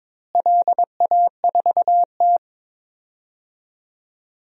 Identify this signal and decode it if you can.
Morse code, heard as a keyed tone, about 22 wpm, 700 Hz. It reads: LA4T
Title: DXCC-100 Call Signs - 4 Characters - Repeated 3x - 22wpm